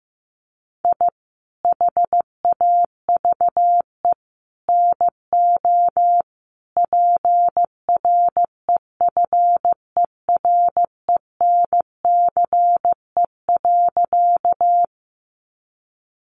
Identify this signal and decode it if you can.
Morse code, heard as a keyed tone, about 15 wpm, 700 Hz. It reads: I HAVE NO PREFERENCE.
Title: Sentences from Top 1500 Words 15wpm